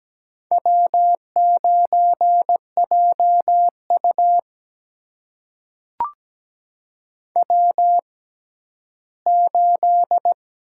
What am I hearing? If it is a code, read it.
Morse code, heard as a keyed tone, about 17 wpm, spaced as for 11 wpm, 700 Hz. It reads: W9JU W8
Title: USA Call Signs - 6x Character Spacing - 17wpm